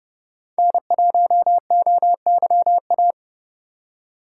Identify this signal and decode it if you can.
Morse code, heard as a keyed tone, about 29 wpm, 700 Hz. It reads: N1OYA